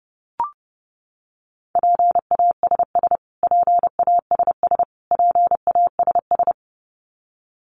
Morse code, heard as a keyed tone, about 30 wpm, 700 Hz. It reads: PASS PASS PASS
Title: Top 500 Words Repeated 30wpm